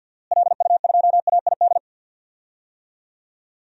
Morse code, heard as a keyed tone, about 51 wpm, 700 Hz. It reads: LU2AID
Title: DXCC-100 Call Signs - 6 Characters - Repeated 3x - 50wpm